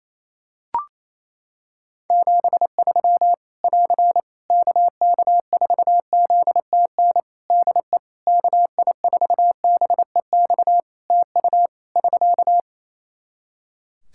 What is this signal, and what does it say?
Morse code, heard as a keyed tone, about 27 wpm, 700 Hz. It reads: 73 <AR> KK4ZTN DE KI4BEX TU <SK>
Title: K1USN SST QSO Practice 28wpm